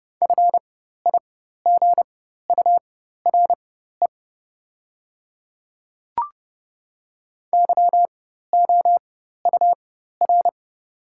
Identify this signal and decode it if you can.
Morse code, heard as a keyed tone, about 30 wpm, spaced as for 14 wpm, 700 Hz. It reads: FIGURE YOUR
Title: 4x Character Spacing - Top 500 Words - 30wpm